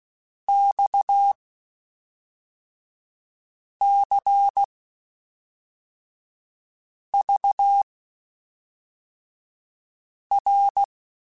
Morse code, heard as a keyed tone, about 16 wpm, spaced as for 3 wpm, 800 Hz. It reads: XCVR
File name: QSOs / QSOs-mp3s / WD0FBT 1@5